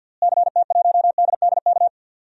Morse code, heard as a keyed tone, about 47 wpm, 700 Hz. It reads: CT1DDK
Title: DXCC-100 Call Signs - 6 Characters - Repeated 3x - 50wpm